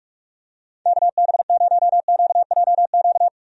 Morse code, heard as a keyed tone, about 46 wpm, 700 Hz. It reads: KB0QJQ
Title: USA Call Signs - 6 Characters - 45wpm